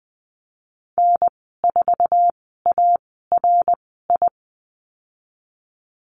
Morse code, heard as a keyed tone, about 20 wpm, 700 Hz. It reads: N 4 A R I
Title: 082 Lesson 14g - Number 2 - Call Signs 20wpm